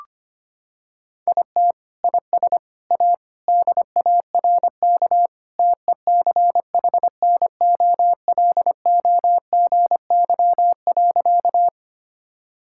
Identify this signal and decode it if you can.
Morse code, heard as a keyed tone, about 25 wpm, 700 Hz. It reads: IT IS A DARK TECHNOLOGY.
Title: Sentences from Top 1000 Words - Review Part 5 - 25wpm